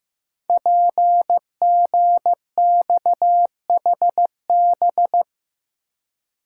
Morse code, heard as a keyed tone, about 15 wpm, 700 Hz. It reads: PGXHB
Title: Code Groups - 5 Letters - 15wpm